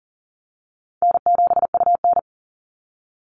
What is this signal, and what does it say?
Morse code, heard as a keyed tone, about 38 wpm, 700 Hz. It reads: N7UN
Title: NA SOTA Activator Callsigns 40wpm